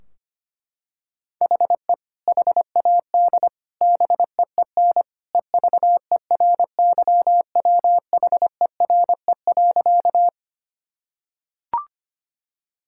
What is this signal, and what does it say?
Morse code, heard as a keyed tone, about 25 wpm, 700 Hz. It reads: HE HAD BEEN EVERYWHERE.
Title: Sentences from Top 1600 Words 25wpm